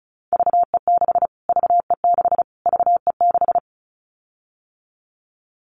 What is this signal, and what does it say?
Morse code, heard as a keyed tone, about 34 wpm, 700 Hz. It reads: VE6 VE6 VE6
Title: Callsign Prefixes for Top 25 DXCC - Repeated 3x - 35wpm